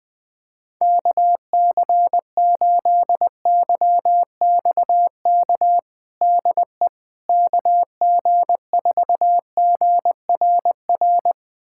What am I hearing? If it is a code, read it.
Morse code, heard as a keyed tone, about 20 wpm, 700 Hz. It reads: KC8YXK DE KG4GRR
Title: USA Callsign DE Callsign - 6 Characters - 20wpm